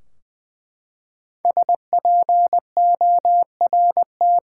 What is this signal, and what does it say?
Morse code, heard as a keyed tone, about 20 wpm, 700 Hz. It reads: SPORT
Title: Five Letter Words - 20wpm